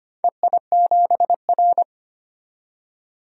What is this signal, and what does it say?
Morse code, heard as a keyed tone, about 25 wpm, 700 Hz. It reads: EI7R
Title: DXCC-100 Call Signs - 4 Characters - 25wpm